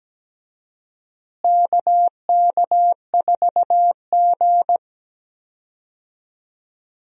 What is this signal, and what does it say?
Morse code, heard as a keyed tone, about 17 wpm, 700 Hz. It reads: KK4G
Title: USA Call Signs - 4 Characters - Repeated 3x - 17wpm